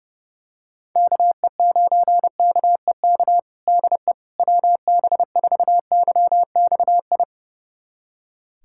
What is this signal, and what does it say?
Morse code, heard as a keyed tone, about 30 wpm, 700 Hz. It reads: KE9KEK DE WB4YXI
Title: USA Callsign DE Callsign - 6 Characters - 30wpm